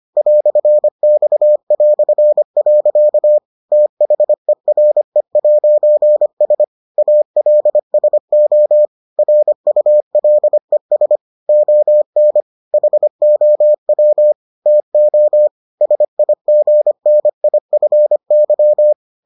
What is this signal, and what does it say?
Morse code, heard as a keyed tone, about 25 wpm, 600 Hz. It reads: "X". THERE'S ALSO RULES ON HOW TO SIGNIFY